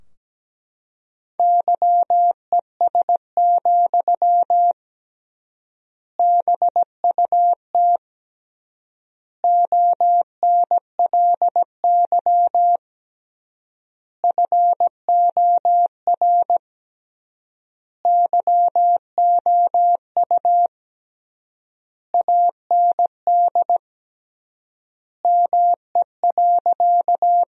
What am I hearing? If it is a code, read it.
Morse code, heard as a keyed tone, about 17 wpm, 700 Hz. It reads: YES, BUT ONLY FOR YOU AND ME.
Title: 3x Word-Spacing - Sentences from Top 500 Words - 17wpm